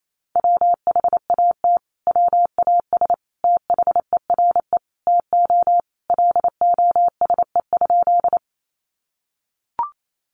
Morse code, heard as a keyed tone, about 28 wpm, 700 Hz. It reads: WHAT WAS THERE TO LOSE?
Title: Sentences from Top 1100 Words 28wpm